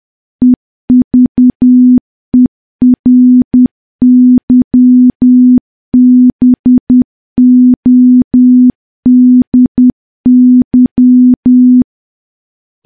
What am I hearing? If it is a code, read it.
Morse code, heard as a keyed tone, about 10 wpm, 250 Hz. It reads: EVERYBODY
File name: morse / morse-training-10WPM